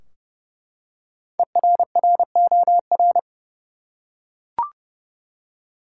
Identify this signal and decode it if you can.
Morse code, heard as a keyed tone, about 32 wpm, 700 Hz. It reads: ERROR
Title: Five Letter Words - 30wpm